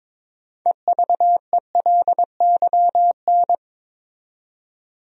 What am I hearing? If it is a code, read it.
Morse code, heard as a keyed tone, about 22 wpm, 700 Hz. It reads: EVELYN
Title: USA Common Names - 22wpm